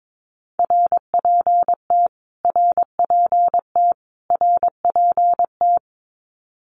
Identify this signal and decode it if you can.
Morse code, heard as a keyed tone, about 22 wpm, 700 Hz. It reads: RPT RPT RPT
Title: Abbreviations Repeated 3x - 22wpm